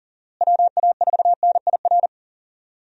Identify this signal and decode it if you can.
Morse code, heard as a keyed tone, about 40 wpm, 700 Hz. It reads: WA4NIR
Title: USA Call Signs - 6 Characters - Repeated 3x - 40wpm